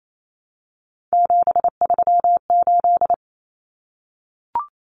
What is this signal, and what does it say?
Morse code, heard as a keyed tone, about 28 wpm, 700 Hz. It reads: 738
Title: Numbers - 3-Digit 28wpm